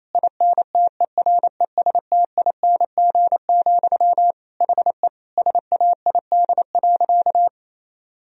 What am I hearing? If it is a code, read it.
Morse code, heard as a keyed tone, about 28 wpm, 700 Hz. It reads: INTERESTING, HE SAID.
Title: Sentences from Top 1000 Words - Review Part 3 - 28wpm